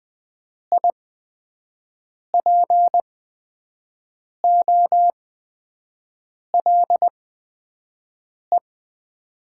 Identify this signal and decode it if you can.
Morse code, heard as a keyed tone, about 20 wpm, spaced as for 5 wpm, 700 Hz. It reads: IPOLE
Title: Top 90 QSO Elements - 8x Character Spacing - 20wpm